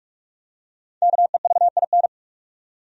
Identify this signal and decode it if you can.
Morse code, heard as a keyed tone, about 47 wpm, 700 Hz. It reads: KEVIN
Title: USA Common Names - Max 5 Letters - 45wpm